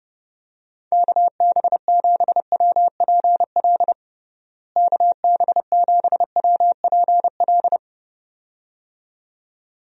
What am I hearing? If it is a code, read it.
Morse code, heard as a keyed tone, about 30 wpm, 700 Hz. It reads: KB7WPL KB7WPL
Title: USA Call Signs - 6 Characters - Repeated 2x - 30wpm